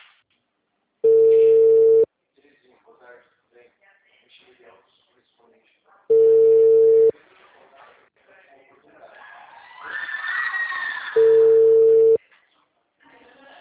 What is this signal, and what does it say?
Ringback tone (China, 450 Hz), with speech.